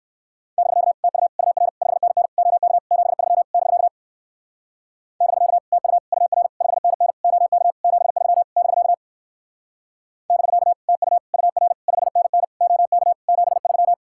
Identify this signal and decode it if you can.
Morse code, heard as a keyed tone, about 57 wpm, 700 Hz. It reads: <BK> TU UR 5NN OK 73 <BK> <BK> TU UR 5NN OK 73 <BK> <BK> TU UR 5NN OK 73